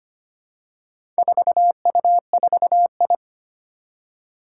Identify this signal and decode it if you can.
Morse code, heard as a keyed tone, about 25 wpm, 700 Hz. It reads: 4U4I
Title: Four Challenging Letter-Number Combinations 25wpm